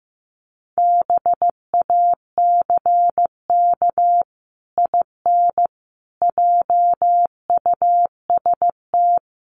Morse code, heard as a keyed tone, about 14 wpm, 700 Hz. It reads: BACK IN JUST
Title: Sets of 5 Words Encore 15wpm